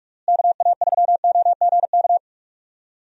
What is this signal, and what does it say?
Morse code, heard as a keyed tone, about 46 wpm, 700 Hz. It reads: KA3OGK
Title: USA Call Signs - 6 Characters - Repeated 3x - 45wpm